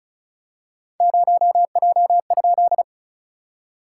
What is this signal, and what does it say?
Morse code, heard as a keyed tone, about 34 wpm, 690 Hz. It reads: 0J?
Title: SOTA QSO Practice 35wpm